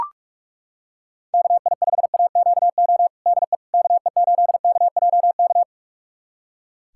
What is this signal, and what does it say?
Morse code, heard as a keyed tone, about 42 wpm, 700 Hz. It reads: KI5AQO DE KE8KJK
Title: USA Callsign DE Callsign 45wpm